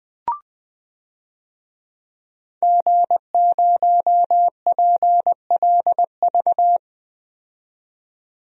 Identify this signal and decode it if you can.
Morse code, heard as a keyed tone, about 20 wpm, 700 Hz. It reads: G0PLV
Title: DXCC-100 Call Signs - 5 Characters - 20wpm